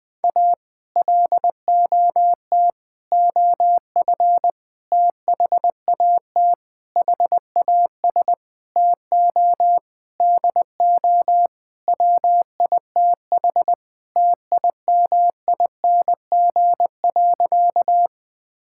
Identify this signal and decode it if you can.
Morse code, heard as a keyed tone, about 20 wpm, 700 Hz. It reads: A LOT OF THAT HAS TO DO WITH TIMING.
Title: Sentences with 8-Word Phrases - 20wpm